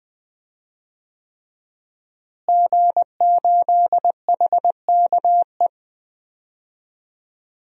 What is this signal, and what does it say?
Morse code, heard as a keyed tone, about 20 wpm, 700 Hz. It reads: G8HKE
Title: DXCC-100 Call Signs - 5 Characters - Repeated 3x - 20wpm